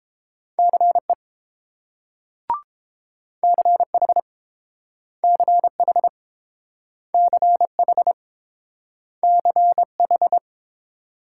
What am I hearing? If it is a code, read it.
Morse code, heard as a keyed tone, about 30 wpm, 700 Hz. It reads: CE CH CH CH CH